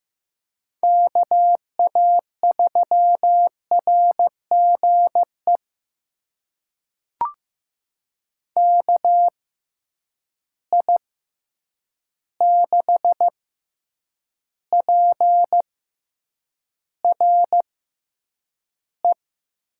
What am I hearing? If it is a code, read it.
Morse code, heard as a keyed tone, about 15 wpm, spaced as for 8 wpm, 700 Hz. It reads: KA3RGE KI6PRE
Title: USA Call Signs - 6 Characters - 6x Character Spacing - 15wpm